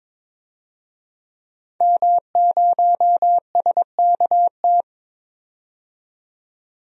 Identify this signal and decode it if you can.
Morse code, heard as a keyed tone, about 22 wpm, 700 Hz. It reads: M0SKT
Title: DXCC-100 Call Signs - 5 Characters - 22wpm